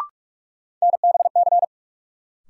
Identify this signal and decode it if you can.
Morse code, heard as a keyed tone, about 42 wpm, 700 Hz. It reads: NBC
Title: Sets of 5 Words Encore - New Word Drill - 45wpm